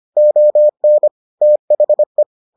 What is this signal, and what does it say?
Morse code, heard as a keyed tone, about 25 wpm, 600 Hz. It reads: ON THE